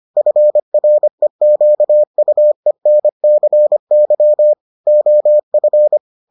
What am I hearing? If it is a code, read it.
Morse code, heard as a keyed tone, about 25 wpm, 600 Hz. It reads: FREQUENCY OF